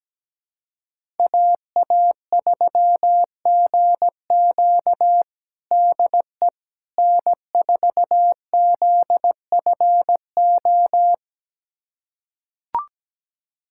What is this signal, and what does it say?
Morse code, heard as a keyed tone, about 17 wpm, 700 Hz. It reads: AA3GQ DE N4ZFO
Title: USA Callsign DE Callsign - 5 Characters - 17wpm